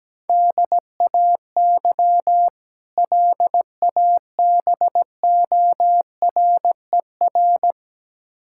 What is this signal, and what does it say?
Morse code, heard as a keyed tone, about 17 wpm, 700 Hz. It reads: DAY LABORER
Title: K1USN SST QSO Practice 17wpm